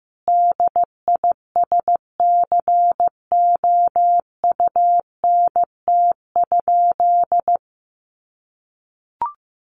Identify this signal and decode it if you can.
Morse code, heard as a keyed tone, about 14 wpm, 700 Hz. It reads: DISCOUNT?
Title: Sentences from Top 1200 Words 15wpm